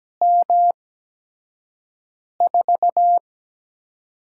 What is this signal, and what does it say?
Morse code, heard as a keyed tone, about 17 wpm, spaced as for 5 wpm, 700 Hz. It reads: M4
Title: USA Call Signs - 6 Characters - 8x Character Spacing - 17wpm